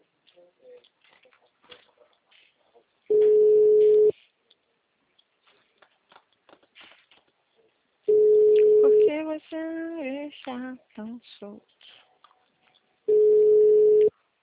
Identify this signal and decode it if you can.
Ringback tone (Europe, 425 Hz).